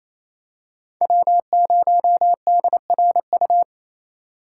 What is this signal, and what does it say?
Morse code, heard as a keyed tone, about 29 wpm, 700 Hz. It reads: W0DRU
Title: USA Call Signs - 5 Characters - 6x Character Spacing - 28wpm